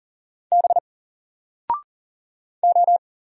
Morse code, heard as a keyed tone, about 39 wpm, 700 Hz. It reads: B O
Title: Single Character (Letter:Number:Punctuation:Prosign) 40wpm